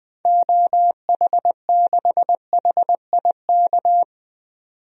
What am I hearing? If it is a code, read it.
Morse code, heard as a keyed tone, about 20 wpm, 700 Hz. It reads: OH6HIK
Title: DXCC-100 Call Signs - 6 Characters - 20wpm